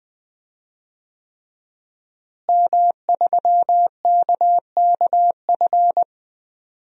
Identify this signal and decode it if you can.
Morse code, heard as a keyed tone, about 20 wpm, 700 Hz. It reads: M3KKF
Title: DXCC-100 Call Signs - 5 Characters - Repeated 3x - 20wpm